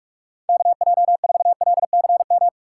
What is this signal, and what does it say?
Morse code, heard as a keyed tone, about 42 wpm, 700 Hz. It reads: KJ4LCM
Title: USA Call Signs - 6 Characters - 4x Character Spacing - 45wpm